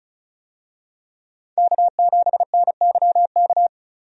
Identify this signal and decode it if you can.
Morse code, heard as a keyed tone, about 34 wpm, 700 Hz. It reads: K7NYK
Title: USA Call Signs - 5 Characters - 35wpm